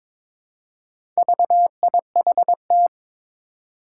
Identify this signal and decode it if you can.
Morse code, heard as a keyed tone, about 22 wpm, 700 Hz. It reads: VIHT
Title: Four Challenging Letter-Number Combinations 22wpm